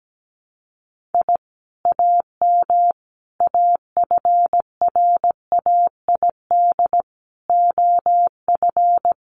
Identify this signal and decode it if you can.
Morse code, heard as a keyed tone, about 17 wpm, 700 Hz. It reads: I AM AFRAID OF
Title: Sentences from Top 1500 Words 17wpm